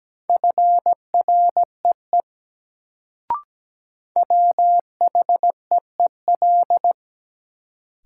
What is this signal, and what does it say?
Morse code, heard as a keyed tone, about 17 wpm, 700 Hz. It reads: FREE WHEEL
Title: Sentences from Top 500 Words - New Word Drill - 17wpm